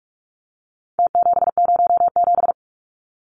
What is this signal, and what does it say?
Morse code, heard as a keyed tone, about 44 wpm, 700 Hz. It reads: T707
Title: Numbers - Contest Style - 4-Digit 45wpm